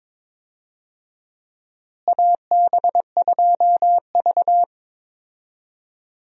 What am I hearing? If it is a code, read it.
Morse code, heard as a keyed tone, about 22 wpm, 700 Hz. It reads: AB2V
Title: USA Call Signs - 4 Characters - Repeated 2x - 22wpm